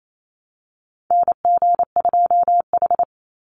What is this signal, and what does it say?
Morse code, heard as a keyed tone, about 28 wpm, 700 Hz. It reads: NG2H
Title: Call Signs Repeated 3x - 28wpm